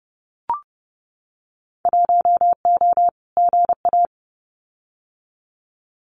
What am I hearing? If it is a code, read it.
Morse code, heard as a keyed tone, about 30 wpm, 700 Hz. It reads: 1O GA
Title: Winter Field Day Exchanges 30wpm